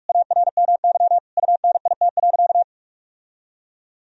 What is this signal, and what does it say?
Morse code, heard as a keyed tone, about 46 wpm, 700 Hz. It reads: ARMY UNIT.
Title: Sentences from Top 1000 Words - Review Part 2 - 45wpm